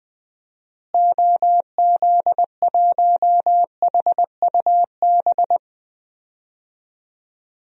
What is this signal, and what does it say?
Morse code, heard as a keyed tone, about 20 wpm, 700 Hz. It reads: OZ1HUB
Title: DXCC-100 Call Signs - 6 Characters - Repeated 2x - 20wpm